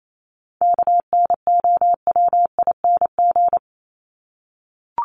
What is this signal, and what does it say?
Morse code, heard as a keyed tone, about 28 wpm, 700 Hz. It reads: KNOWING
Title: ING Suffix - Top 100 Words with Base Words 28wpm